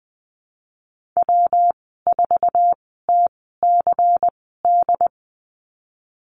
Morse code, heard as a keyed tone, about 19 wpm, spaced as for 14 wpm, 700 Hz. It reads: W4TCD
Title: 128 Lesson 21g - Number 6 - Call Signs 20wpm